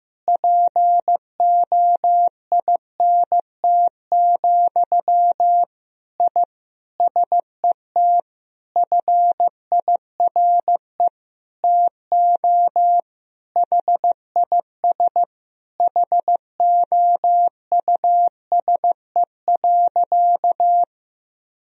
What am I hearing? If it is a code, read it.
Morse code, heard as a keyed tone, about 16 wpm, 700 Hz. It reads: POINT, I SET FIRE TO HIS HOUSE.
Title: Sentences from Top 1000 Words - Review Part 1 - 15wpm